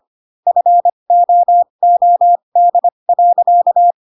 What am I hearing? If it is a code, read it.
Morse code, heard as a keyed tone, about 25 wpm, spaced as for 22 wpm, 700 Hz. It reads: FOOD.
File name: Pinocchio / Pinocchio ch7 18 25 ik1hge